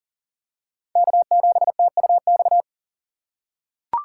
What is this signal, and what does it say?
Morse code, heard as a keyed tone, about 40 wpm, 700 Hz. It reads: K7TUX
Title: USA Call Signs - 5 Characters - 4x Character Spacing - 40wpm